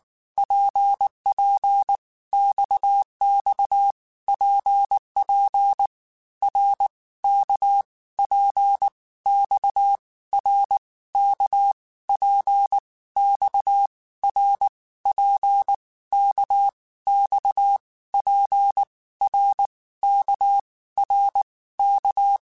Morse code, heard as a keyed tone, about 19 wpm, 800 Hz. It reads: PP XX PP R K P X R K P X R P K X P R K R K
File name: Basic Letter Groups By7Groups / RKPX / RKPX@18